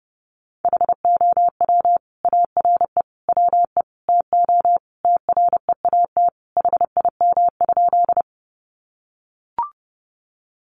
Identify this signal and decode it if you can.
Morse code, heard as a keyed tone, about 30 wpm, 700 Hz. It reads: HOW ARE WE TO TREAT HIM?
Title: Sentences from Top 1300 Words 30wpm